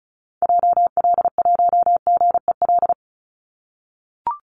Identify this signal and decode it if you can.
Morse code, heard as a keyed tone, about 34 wpm, 700 Hz. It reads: JL1GEL
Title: Call Signs - 35wpm